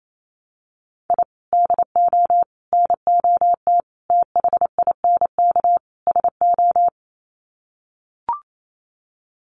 Morse code, heard as a keyed tone, about 28 wpm, 700 Hz. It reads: I DO NOT THINK SO
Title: Sets of 5 Words 28wpm